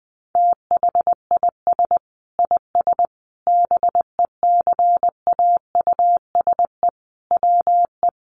Morse code, heard as a keyed tone, about 19 wpm, 700 Hz. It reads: THIS IS BECAUSE WE
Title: Sentences from Top 1200 Words 20wpm